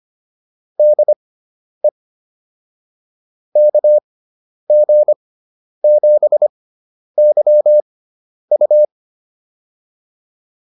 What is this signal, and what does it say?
Morse code, heard as a keyed tone, about 25 wpm, spaced as for 10 wpm, 600 Hz. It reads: DE KG7YU